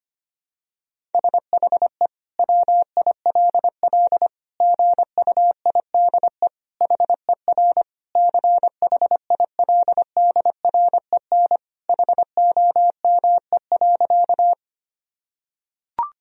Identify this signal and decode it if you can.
Morse code, heard as a keyed tone, about 25 wpm, 700 Hz. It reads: SHE WILL GUIDE HER CHILDREN HOME.
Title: Sentences from Top 1000 Words - Review Part 3 - 25wpm